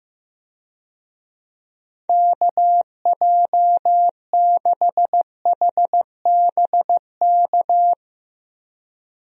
Morse code, heard as a keyed tone, about 15 wpm, 700 Hz. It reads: KJ6HBK
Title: USA Call Signs - 6 Characters - Repeated 3x - 15wpm